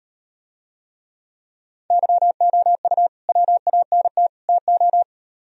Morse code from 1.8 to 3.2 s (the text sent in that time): YOU